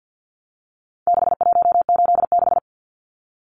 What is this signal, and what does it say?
Morse code, heard as a keyed tone, about 50 wpm, 700 Hz. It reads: 6186
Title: Numbers - 4-Digit 50wpm